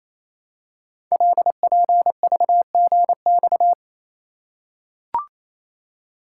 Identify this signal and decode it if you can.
Morse code, heard as a keyed tone, about 28 wpm, 700 Hz. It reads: LPVGX
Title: Code Groups - 5 Letters - 28wpm